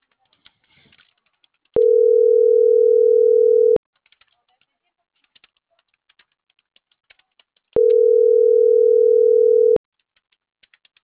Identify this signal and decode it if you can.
Ringback tone (North America), 440 + 480 Hz.